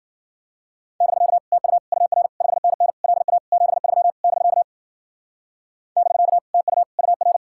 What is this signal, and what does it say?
Morse code, heard as a keyed tone, about 45 wpm, 700 Hz. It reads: <BK> TU UR 5NN LA 73 <BK> <BK> TU UR
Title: Speed-Racing Common QSO Phrases Encore - 40wpm